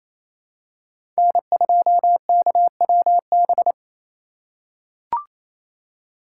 Morse code, heard as a keyed tone, about 28 wpm, 700 Hz. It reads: N2KWB
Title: USA Call Signs - 5 Characters - 2x Character Spacing - 28wpm